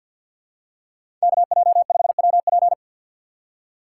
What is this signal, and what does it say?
Morse code, heard as a keyed tone, about 51 wpm, 700 Hz. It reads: KJ5WP
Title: USA Call Signs - 5 Characters - 4x Character Spacing - 50wpm